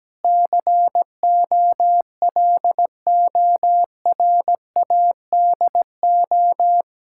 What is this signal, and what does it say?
Morse code, heard as a keyed tone, about 17 wpm, 700 Hz. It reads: COLORADO
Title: Practice QSO - Exchange 1 - Beginning - 17wpm